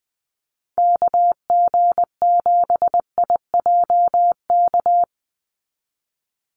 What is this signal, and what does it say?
Morse code, heard as a keyed tone, about 19 wpm, 700 Hz. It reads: KG7IJK
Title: USA Call Signs 20wpm